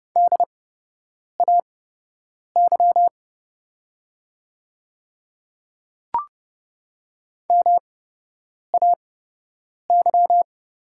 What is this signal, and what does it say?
Morse code, heard as a keyed tone, about 30 wpm, spaced as for 8 wpm, 700 Hz. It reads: DAY MAY
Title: 8x Character Spacing - Top 200 Words - 30wpm